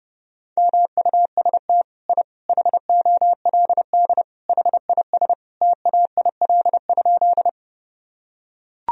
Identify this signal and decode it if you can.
Morse code, heard as a keyed tone, about 32 wpm, 700 Hz. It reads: MUST I HOLD HIS TAIL?
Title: Sentences from Top 1000 Words - Review Part 3 - 30wpm